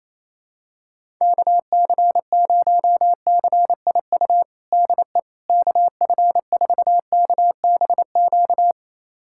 Morse code, heard as a keyed tone, about 28 wpm, 700 Hz. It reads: KC0CIU DE KF4KBQ
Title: USA Callsign DE Callsign - 6 Characters - 28wpm